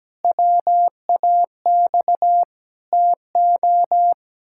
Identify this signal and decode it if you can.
Morse code, heard as a keyed tone, about 17 wpm, 700 Hz. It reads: WAX TO
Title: Pangram Challenge v2 - 17wpm